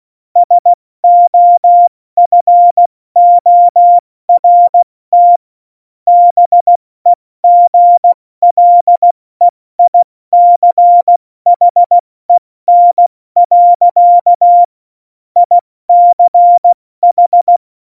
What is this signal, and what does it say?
Morse code, heard as a keyed tone, about 16 wpm, 700 Hz. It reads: SOFORT BEGLEICHEN. ICH
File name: Morsekurs / Witze / Witz 02 Juni 2020, 70 F80